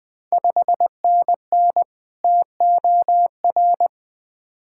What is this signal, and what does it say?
Morse code, heard as a keyed tone, about 20 wpm, 700 Hz. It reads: 5NN TOR
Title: Ontario QSO Party Exchange 20wpm